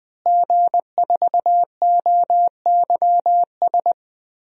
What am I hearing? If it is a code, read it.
Morse code, heard as a keyed tone, about 20 wpm, 700 Hz. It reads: G4OYS
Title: DXCC-100 Call Signs - 5 Characters - Repeated 2x - 20wpm